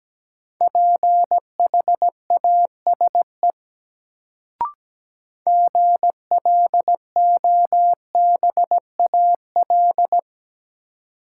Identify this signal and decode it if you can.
Morse code, heard as a keyed tone, about 17 wpm, 700 Hz. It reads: PHASE GLOBAL
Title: Sentences from Top 1200 Words - New Word Drill Alt - 17wpm